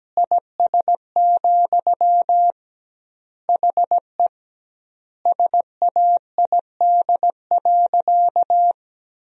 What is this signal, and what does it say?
Morse code, heard as a keyed tone, about 17 wpm, 700 Hz. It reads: IS, HE SAID.
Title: 2x Word-Spacing - Sentences from Top 200 Words - 17wpm